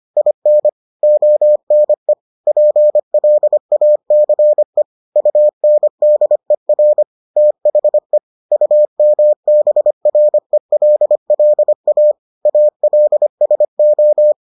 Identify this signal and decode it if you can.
Morse code, heard as a keyed tone, about 25 wpm, 600 Hz. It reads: IN ONE PLACE UNDER THE UMBRELLA ALSO